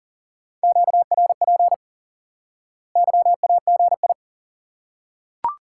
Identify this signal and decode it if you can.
Morse code, heard as a keyed tone, about 40 wpm, 700 Hz. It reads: QRP YAGI
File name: IQE Course 3 - Lesson 89 - YAGI - Two Elements - 40wpm